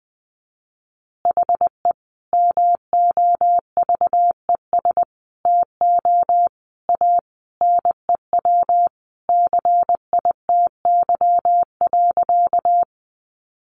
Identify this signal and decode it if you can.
Morse code, heard as a keyed tone, about 20 wpm, 700 Hz. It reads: HE MOVES TO A NEW CITY.
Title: Easy Phrases 20wpm